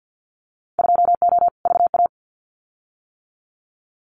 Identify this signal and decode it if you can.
Morse code, heard as a keyed tone, about 50 wpm, 700 Hz. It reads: 2O VA
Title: Winter Field Day Exchanges 50wpm